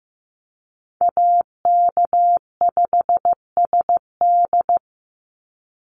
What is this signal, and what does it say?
Morse code, heard as a keyed tone, about 15 wpm, 700 Hz. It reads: AK5SD
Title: NA SOTA Chaser Callsigns 15wpm